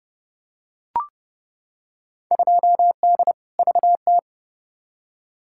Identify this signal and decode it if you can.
Morse code, heard as a keyed tone, about 30 wpm, 700 Hz. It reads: 2D VT
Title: Field Day Exchanges Encore - 30wpm